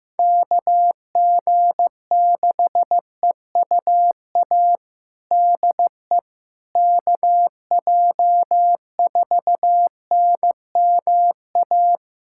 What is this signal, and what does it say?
Morse code, heard as a keyed tone, about 15 wpm, 700 Hz. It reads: KG6EUA DE KJ4NMA
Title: USA Callsign DE Callsign - 6 Characters - 15wpm